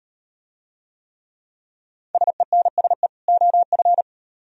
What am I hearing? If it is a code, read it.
Morse code, heard as a keyed tone, about 35 wpm, 700 Hz. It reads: SENSE OF